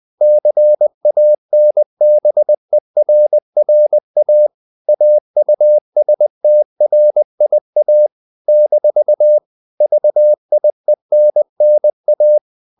Morse code, heard as a keyed tone, about 20 wpm, 600 Hz. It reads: CANBERRA AUSTRIA - VIENNA
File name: CW / Practice / NationalCapitals20